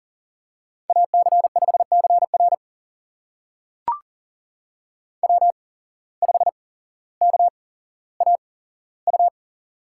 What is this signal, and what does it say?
Morse code, heard as a keyed tone, about 39 wpm, spaced as for 17 wpm, 700 Hz. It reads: AC5CR W5KAU
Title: USA Call Signs - 5 Characters - 8x Character Spacing - 40wpm